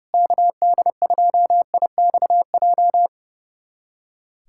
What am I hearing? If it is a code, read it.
Morse code, heard as a keyed tone, about 32 wpm, 700 Hz. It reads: KD2IXJ
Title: USA Callsign DE Callsign - 6 Characters - 30wpm